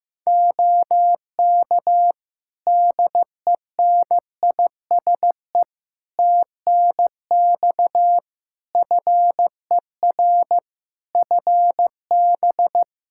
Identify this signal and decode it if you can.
Morse code, heard as a keyed tone, about 15 wpm, 700 Hz. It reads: OK DENISE TNX FER FB
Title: K1USN SST QSO Practice 15wpm